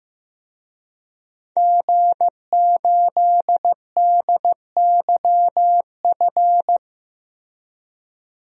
Morse code, heard as a keyed tone, about 15 wpm, 700 Hz. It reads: G8DYF
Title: DXCC-100 Call Signs - 5 Characters - 15wpm